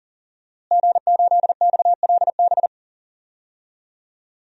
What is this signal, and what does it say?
Morse code, heard as a keyed tone, about 40 wpm, 700 Hz. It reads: G8XLB
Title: DXCC-100 Call Signs - 5 Characters - Repeated 3x - 40wpm